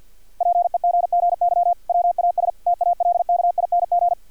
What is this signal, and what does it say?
Morse code, heard as a keyed tone, about 50 wpm, 700 Hz. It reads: PEGGY WAS TALKING
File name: archived / CRSnet